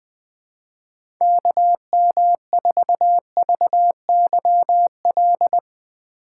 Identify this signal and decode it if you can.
Morse code, heard as a keyed tone, about 20 wpm, 700 Hz. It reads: KM4VYL
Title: USA Call Signs - 6 Characters - 4x Character Spacing - 20wpm